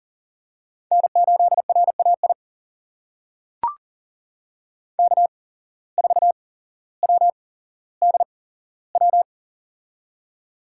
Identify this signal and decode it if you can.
Morse code, heard as a keyed tone, about 39 wpm, spaced as for 17 wpm, 700 Hz. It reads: N8RAI K4WDW
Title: USA Call Signs - 5 Characters - 8x Character Spacing - 40wpm